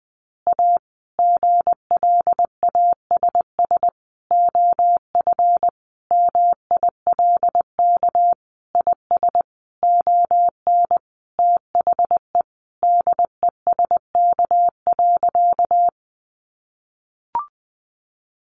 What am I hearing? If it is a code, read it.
Morse code, heard as a keyed tone, about 20 wpm, 700 Hz. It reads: A GLASS OF MILK IS ON THE DESK.
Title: Additional Easy Phrases 20wpm